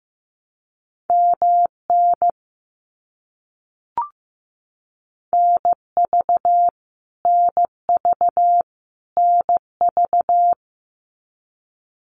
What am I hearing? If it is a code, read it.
Morse code, heard as a keyed tone, about 15 wpm, 700 Hz. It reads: MN NV NV NV
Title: US States Repeated 3x 15wpm